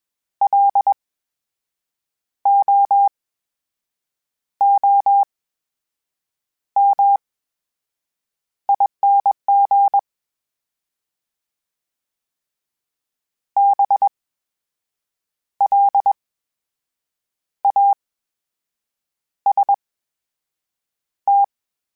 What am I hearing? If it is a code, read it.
Morse code, heard as a keyed tone, about 21 wpm, spaced as for 6 wpm, 800 Hz. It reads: LOOMING BLAST
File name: Special INGs-mp3s / Special INGs-mp3s / VerySlowToFaster Ing @20